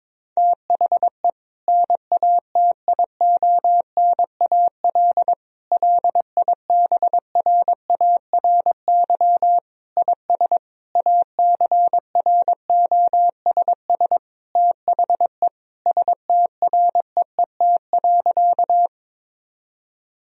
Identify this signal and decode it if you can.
Morse code, heard as a keyed tone, about 21 wpm, 700 Hz. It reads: THE NATIONAL LIBRARY IS ACROSS THE STREET.
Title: Sentences from Top 1000 Words - Review Part 4 - 22wpm